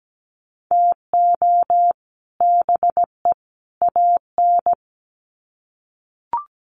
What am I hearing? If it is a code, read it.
Morse code, heard as a keyed tone, about 17 wpm, 700 Hz. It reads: TO BE AN
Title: Sets of 3 Words 17wpm